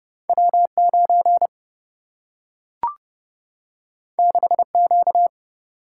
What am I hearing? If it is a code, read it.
Morse code, heard as a keyed tone, about 30 wpm, 700 Hz. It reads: W9 6Q
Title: Single-Letter Single-Number 30wpm